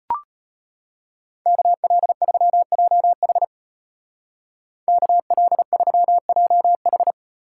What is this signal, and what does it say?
Morse code, heard as a keyed tone, about 33 wpm, 700 Hz. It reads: KL3JH KL3JH